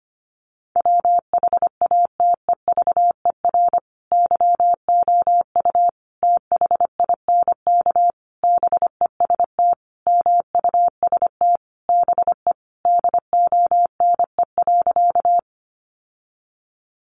Morse code, heard as a keyed tone, about 25 wpm, 700 Hz. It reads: WHATEVER YOU THINK BEST MUST BE DONE.
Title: Sentences from Top 1100 Words 25wpm